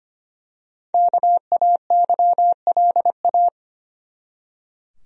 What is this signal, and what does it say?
Morse code, heard as a keyed tone, about 25 wpm, 700 Hz. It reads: KAYLA
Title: USA Common Names - Max 5 Letters - 25wpm